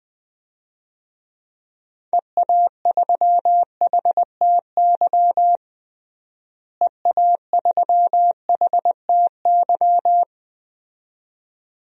Morse code, heard as a keyed tone, about 21 wpm, 700 Hz. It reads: EA3HTY EA3HTY
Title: DXCC-100 Call Signs - 6 Characters - Repeated 2x - 20wpm